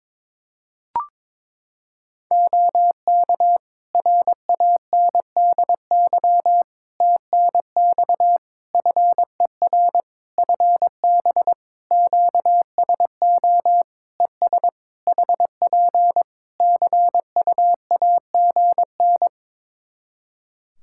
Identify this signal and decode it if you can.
Morse code, heard as a keyed tone, about 22 wpm, 700 Hz. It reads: OK RANDY TNX FER FB QSO ES HP CUAGN
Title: Practice QSO - Exchange 4 - Ending - 22wpm